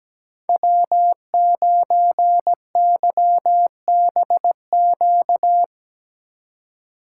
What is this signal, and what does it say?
Morse code, heard as a keyed tone, about 17 wpm, 700 Hz. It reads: W9YBQ
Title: USA Call Signs - 2x Character Spacing - 17wpm